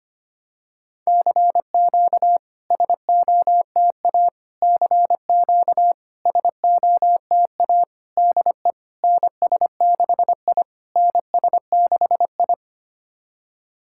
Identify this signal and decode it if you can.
Morse code, heard as a keyed tone, about 25 wpm, 700 Hz. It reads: CQ SOTA CQ SOTA DE NS6I NS6I
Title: SOTA QSO Practice 25wpm